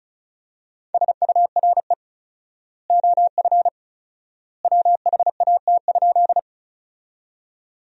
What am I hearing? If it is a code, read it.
Morse code, heard as a keyed tone, about 34 wpm, 700 Hz. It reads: SURE OF WHAT?
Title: 4x Word-Spacing - Sentences from Top 300 Words - 35wpm